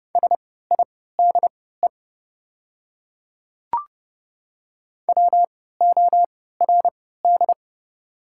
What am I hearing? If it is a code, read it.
Morse code, heard as a keyed tone, about 30 wpm, spaced as for 17 wpm, 700 Hz. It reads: SIDE WORD
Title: 3x Character Spacing - Top 300 Words - 30wpm